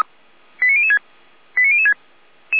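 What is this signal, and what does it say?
Five-tone selective call (ZVEI), heard as quick runs of five tones, tones of about 80 ms. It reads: 89006 89006
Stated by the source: ZVEI five-tone call 89001 received by SDR, from feuerwehr-orlamuende.org (Orlamünde fire brigade) audio archive.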